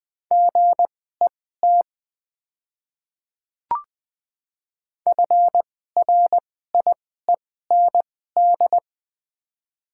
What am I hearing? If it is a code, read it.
Morse code, heard as a keyed tone, about 20 wpm, spaced as for 14 wpm, 700 Hz. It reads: GET FRIEND
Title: 2x Character Spacing - Top 300 Words - 20wpm